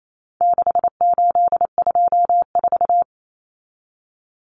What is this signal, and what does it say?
Morse code, heard as a keyed tone, about 28 wpm, 700 Hz. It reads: 6824
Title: Common Cut Numbers - 4-Digit 28wpm